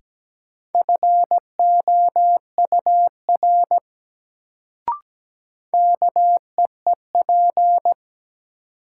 Morse code, heard as a keyed tone, about 17 wpm, 700 Hz. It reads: FOUR KEEP
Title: Sentences from Top 300 Words - New Word Drill - 17wpm